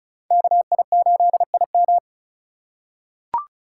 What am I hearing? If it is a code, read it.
Morse code, heard as a keyed tone, about 36 wpm, 700 Hz. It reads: KI8IM
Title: USA Call Signs - 3x Character Spacing - 35wpm